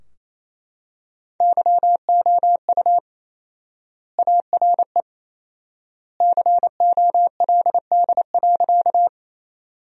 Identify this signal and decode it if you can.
Morse code, heard as a keyed tone, about 28 wpm, 700 Hz. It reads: YOU ARE COLD.
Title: 4x Word-Spacing - Sentences from Top 400 Words - 28wpm